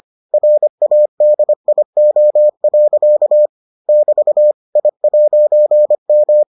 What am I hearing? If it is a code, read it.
Morse code, heard as a keyed tone, about 25 wpm, 600 Hz. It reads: RADIO. <BT> I'M